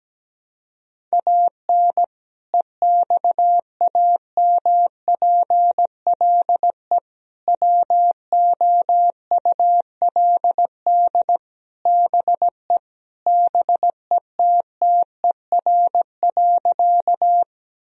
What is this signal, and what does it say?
Morse code, heard as a keyed tone, about 17 wpm, 700 Hz. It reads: AN EXAMPLE WOULD BE BETTER.
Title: Sentences from Top 1000 Words - Review Part 1 - 17wpm